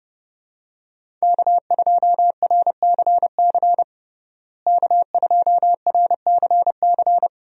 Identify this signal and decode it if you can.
Morse code, heard as a keyed tone, about 30 wpm, 700 Hz. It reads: K2RCC K2RCC
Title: USA Call Signs - 5 Characters - Repeated 2x - 30wpm